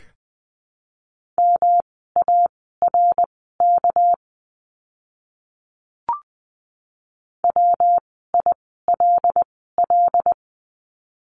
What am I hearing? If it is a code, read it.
Morse code, heard as a keyed tone, about 20 wpm, spaced as for 14 wpm, 700 Hz. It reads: MARK WILL
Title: 199 Lesson 32f - Letter k - Words 20wpm